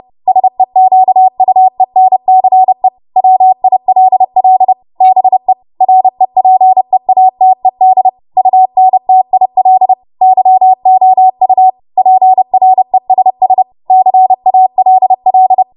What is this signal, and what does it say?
Morse code, heard as a keyed tone, about 30 wpm, 750 Hz. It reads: SEQUENCE WILL BE REPEATED UNTIL YOU PRESS CALL
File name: Morse / 30WPM